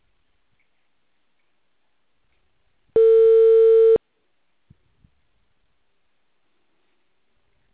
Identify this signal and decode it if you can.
Ringback tone (China, 450 Hz).